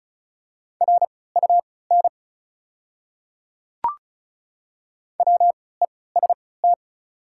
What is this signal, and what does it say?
Morse code, heard as a keyed tone, about 36 wpm, spaced as for 20 wpm, 690 Hz. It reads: RUN WEST
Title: 3x Character Spacing - Top 500 Words - 35wpm